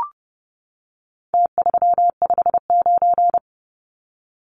Morse code, heard as a keyed tone, about 30 wpm, 700 Hz. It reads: T359
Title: Numbers - Contest Style - 4-Digit 30wpm